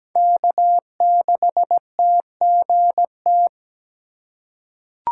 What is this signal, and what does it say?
Morse code, heard as a keyed tone, about 17 wpm, 700 Hz. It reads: K6TGT
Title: USA Call Signs - 5 Characters - 3x Character Spacing - 17wpm